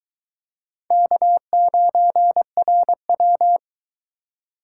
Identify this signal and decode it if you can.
Morse code, heard as a keyed tone, about 18 wpm, 700 Hz. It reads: K9RW
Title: Speed Racing - USA 1x2 Call Signs 15wpm